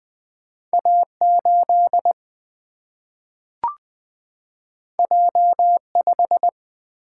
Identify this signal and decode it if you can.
Morse code, heard as a keyed tone, about 20 wpm, 700 Hz. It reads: A8 J5
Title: Single-Letter Single-Number 20wpm